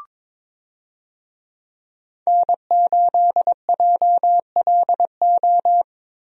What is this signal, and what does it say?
Morse code, heard as a keyed tone, about 21 wpm, 700 Hz. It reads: N8JLO
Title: USA Call Signs - 5 Characters - Repeated 3x - 22wpm